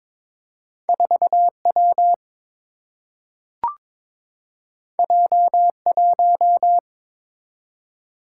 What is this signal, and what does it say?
Morse code, heard as a keyed tone, about 22 wpm, 700 Hz. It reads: 4 W J 1
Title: Single-Letter Single-Number 22wpm